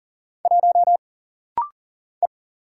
Morse code, heard as a keyed tone, about 39 wpm, 700 Hz. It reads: 1 E
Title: Single Letter-Number - Rapid-Fire 40wpm